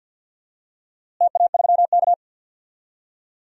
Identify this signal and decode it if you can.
Morse code, heard as a keyed tone, about 49 wpm, 700 Hz. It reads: TA3K
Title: DXCC-100 Call Signs - 4 Characters - Repeated 3x - 50wpm